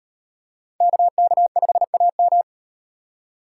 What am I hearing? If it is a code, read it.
Morse code, heard as a keyed tone, about 38 wpm, 700 Hz. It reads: KK5AM